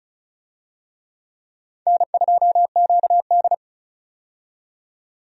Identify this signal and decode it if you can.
Morse code, heard as a keyed tone, about 36 wpm, 700 Hz. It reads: N2QD
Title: USA Call Signs - 4 Characters - 35wpm